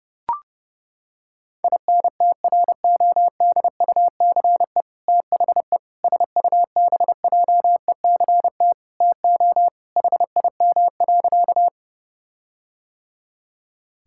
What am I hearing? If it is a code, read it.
Morse code, heard as a keyed tone, about 30 wpm, 700 Hz. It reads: INTRODUCE THE SUBJECT TO HIM.
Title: Sentences from Top 1600 Words 30wpm